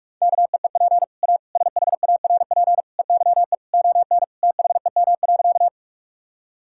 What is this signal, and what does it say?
Morse code, heard as a keyed tone, about 46 wpm, 700 Hz. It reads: KEEP A SHARP EYE ON THEM.
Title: Sentences from Top 1600 Words 45wpm